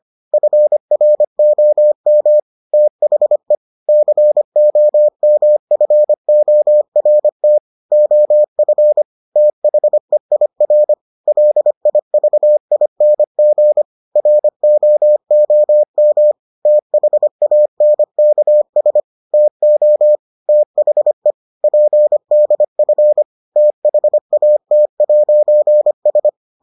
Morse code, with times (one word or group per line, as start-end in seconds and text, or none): 0.3-2.4 FROM
2.7-3.5 THE
3.9-7.6 COMFORT
7.9-9.0 OF
9.4-10.9 THEIR
11.3-13.8 LIVING
14.2-16.3 ROOM
16.7-19.0 THANKS
19.3-20.2 TO
20.5-21.3 THE
21.6-23.2 PDF
23.6-26.3 THAT'S